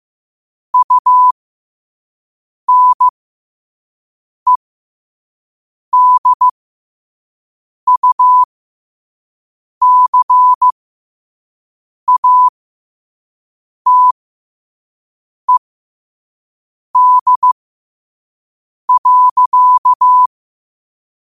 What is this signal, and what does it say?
Morse code, heard as a keyed tone, about 15 wpm, spaced as for 5 wpm, 1000 Hz. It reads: UNEDUCATED.
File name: qotdfiles / 5wpm